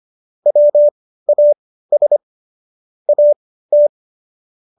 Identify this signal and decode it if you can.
Morse code, heard as a keyed tone, about 25 wpm, spaced as for 15 wpm, 590 Hz. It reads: WAS AT